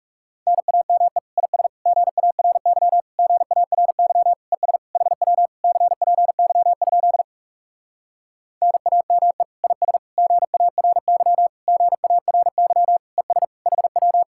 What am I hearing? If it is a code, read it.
Morse code, heard as a keyed tone, about 40 wpm, 700 Hz. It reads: NAME IS GARY GARY ES HW CPY? NAME IS GARY GARY ES HW